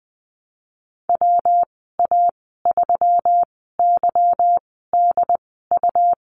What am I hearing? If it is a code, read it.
Morse code, heard as a keyed tone, about 20 wpm, spaced as for 15 wpm, 700 Hz. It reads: WA3YDU
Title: 167 Lesson 27g - Number 7 - Call Signs 20wpm